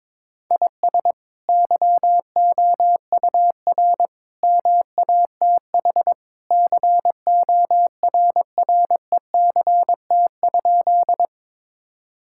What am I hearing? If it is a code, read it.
Morse code, heard as a keyed tone, about 22 wpm, 700 Hz. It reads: IS YOUR MATH CORRECT?
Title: Sentences from Top 1000 Words - Review Part 6 - 22wpm